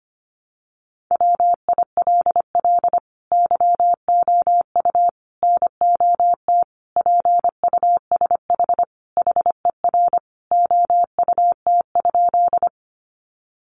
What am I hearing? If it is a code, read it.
Morse code, heard as a keyed tone, about 25 wpm, 700 Hz. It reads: WILL YOU NOT PUSH HER OUT?
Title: Sentences from Top 1300 Words 25wpm